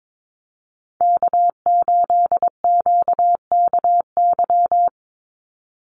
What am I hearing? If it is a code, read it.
Morse code, heard as a keyed tone, about 22 wpm, 700 Hz. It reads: K8QKY
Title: Call Signs Repeated 2x - 22wpm